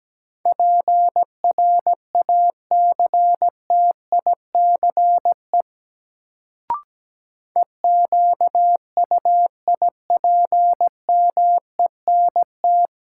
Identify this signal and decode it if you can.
Morse code, heard as a keyed tone, about 17 wpm, 700 Hz. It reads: PRACTICE EQUIPMENT
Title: Sentences from Top 700 Words - New Word Drill - 17wpm